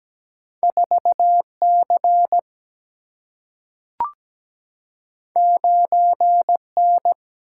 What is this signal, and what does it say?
Morse code, heard as a keyed tone, about 17 wpm, 700 Hz. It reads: 4 C 9 N
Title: Single-Letter Single-Number 17wpm